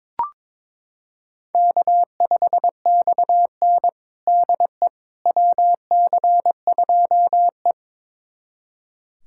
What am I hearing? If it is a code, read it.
Morse code, heard as a keyed tone, about 21 wpm, 700 Hz. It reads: K5XN DE WC2E
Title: USA Callsign DE Callsign - 4 Characters - 22wpm